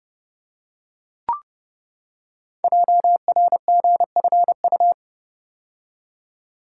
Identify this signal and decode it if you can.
Morse code, heard as a keyed tone, about 30 wpm, 700 Hz. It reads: JRGFU
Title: Code Groups - 5 Letters - 30wpm